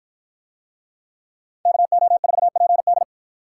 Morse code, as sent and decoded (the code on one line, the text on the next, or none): -.- --- ....- .--. -..
KO4PD